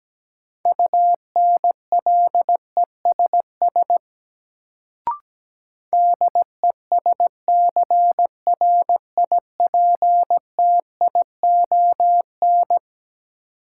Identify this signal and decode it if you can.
Morse code, heard as a keyed tone, about 17 wpm, 700 Hz. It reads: UNLESS DESCRIPTION
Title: Sentences from Top 1300 Words - New Word Drill Alt - 17wpm